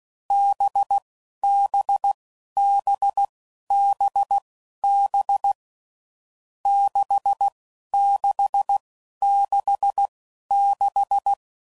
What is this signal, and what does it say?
Morse code, heard as a keyed tone, about 16 wpm, spaced as for 11 wpm, 800 Hz. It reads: BBBBB 6666
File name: Combinations / Difficult Combos / NDB NDB6